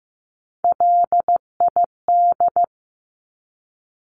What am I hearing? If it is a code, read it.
Morse code, heard as a keyed tone, about 15 wpm, 700 Hz. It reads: LID
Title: Abbreviations 15wpm